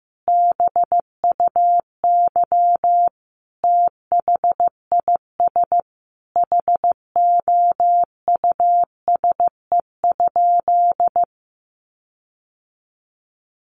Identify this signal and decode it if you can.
Morse code, heard as a keyed tone, about 15 wpm, 700 Hz. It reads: BUY THIS HOUSE?
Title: Sentences from Top 1400 Words 15wpm